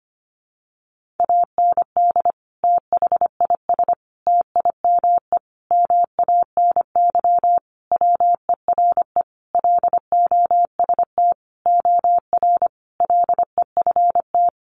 Morse code, heard as a keyed tone, about 25 wpm, 700 Hz. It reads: AND THIS TIME MANY WERE LOST OR LEFT
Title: Sentences from Top 1400 Words 25wpm